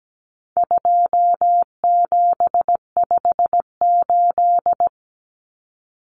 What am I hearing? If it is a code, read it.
Morse code, heard as a keyed tone, about 17 wpm, 700 Hz. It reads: 2758
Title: Numbers - 4-Digit 17wpm